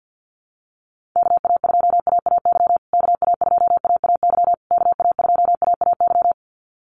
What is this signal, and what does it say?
Morse code, heard as a keyed tone, about 50 wpm, 700 Hz. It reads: KA2AAY KA2AAY KA2AAY
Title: USA Call Signs Repeated 3x 50wpm